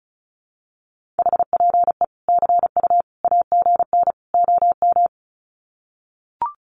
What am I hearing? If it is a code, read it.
Morse code, heard as a keyed tone, about 34 wpm, 700 Hz. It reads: HPE CU AGN OM
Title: Common QSO Phrases 35wpm